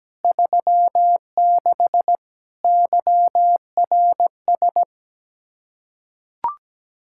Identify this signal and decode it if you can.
Morse code, heard as a keyed tone, about 17 wpm, 700 Hz. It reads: 36 YRS
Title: Practice QSO - Exchange 3 - More Info - 17wpm